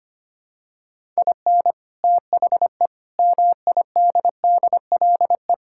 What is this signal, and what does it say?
Morse code, heard as a keyed tone, about 25 wpm, 700 Hz. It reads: IN THE MIDDLE
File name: practice / 25wpm / Sentences with 2-Word Phrases - 25wpm